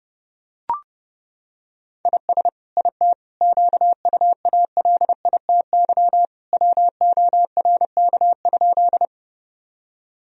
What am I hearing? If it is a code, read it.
Morse code, heard as a keyed tone, about 30 wpm, 700 Hz. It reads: IS IT QUALITY WORK?
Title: Sentences from Top 1000 Words - Review Part 4 - 30wpm